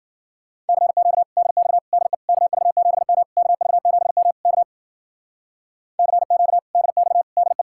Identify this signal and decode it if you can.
Morse code, heard as a keyed tone, about 48 wpm, 700 Hz. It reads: CQ DX DE KV7M KV7M K CQ DX DE
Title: Speed-Racing Common QSO Phrases - 40wpm